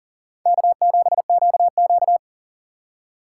Morse code, heard as a keyed tone, about 39 wpm, 700 Hz. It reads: K7QQ
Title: USA Call Signs - 4 Characters - 2x Character Spacing - 40wpm